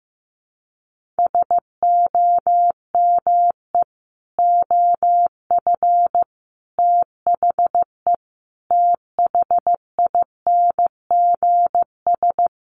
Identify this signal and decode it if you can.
Morse code, heard as a keyed tone, about 15 wpm, 700 Hz. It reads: SOME OF THE THINGS
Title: Sets of 5 Words 15wpm